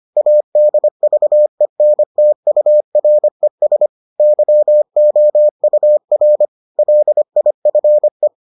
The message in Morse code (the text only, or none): ADVENTURES YOUR LIFE